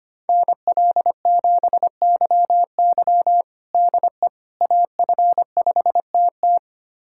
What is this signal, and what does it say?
Morse code, heard as a keyed tone, about 25 wpm, 700 Hz. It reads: NL7YY DE AF5TT
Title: USA Callsign DE Callsign - 5 Characters - 25wpm